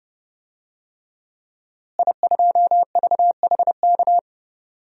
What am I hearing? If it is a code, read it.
Morse code, heard as a keyed tone, about 31 wpm, 700 Hz. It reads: I2VHK
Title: DXCC-100 Call Signs - 5 Characters - Repeated 3x - 30wpm